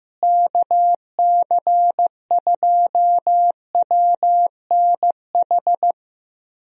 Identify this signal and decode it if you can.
Morse code, heard as a keyed tone, about 15 wpm, 700 Hz. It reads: KC2WNH
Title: USA Call Signs - 6 Characters - Repeated 3x - 15wpm